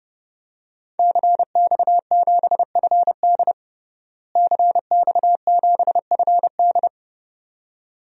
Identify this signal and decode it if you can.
Morse code, heard as a keyed tone, about 32 wpm, 700 Hz. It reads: CX7FD CX7FD
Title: DXCC-100 Call Signs - 5 Characters - Repeated 2x - 30wpm